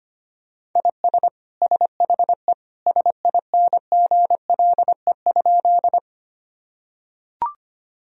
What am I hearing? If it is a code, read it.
Morse code, heard as a keyed tone, about 25 wpm, 700 Hz. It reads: IS SHE SINGLE?
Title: Sentences from Top 1000 Words - Review Part 4 - 25wpm